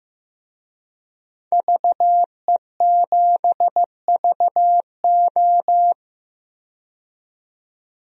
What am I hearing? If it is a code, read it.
Morse code, heard as a keyed tone, about 15 wpm, 700 Hz. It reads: VE7VO
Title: DXCC-100 Call Signs - 5 Characters - 15wpm